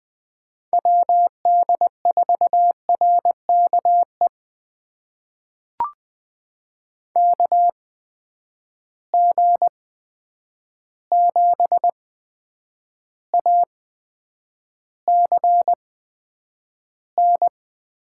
Morse code, heard as a keyed tone, about 20 wpm, spaced as for 9 wpm, 700 Hz. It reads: WD4RKE KG7ACN
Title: USA Call Signs - 6 Characters - 8x Character Spacing - 20wpm